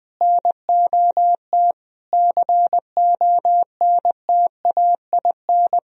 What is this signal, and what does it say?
Morse code, heard as a keyed tone, about 21 wpm, 700 Hz. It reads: NOT CONTAIN
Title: Sentences from Top 1000 Words - Review Part 2 - 20wpm